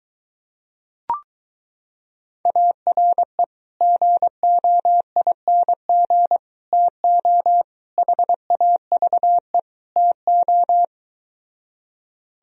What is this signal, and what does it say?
Morse code, heard as a keyed tone, about 23 wpm, 700 Hz. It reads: ARE GOING TO HAVE TO